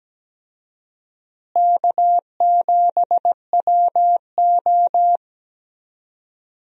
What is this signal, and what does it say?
Morse code, heard as a keyed tone, about 17 wpm, 700 Hz. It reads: K7WO
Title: USA Call Signs - 4 Characters - Repeated 2x - 17wpm